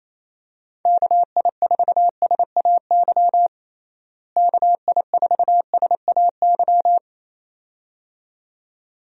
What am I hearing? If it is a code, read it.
Morse code, heard as a keyed tone, about 27 wpm, 700 Hz. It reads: KI4SAY KI4SAY
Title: USA Call Signs - 6 Characters - Repeated 2x - 28wpm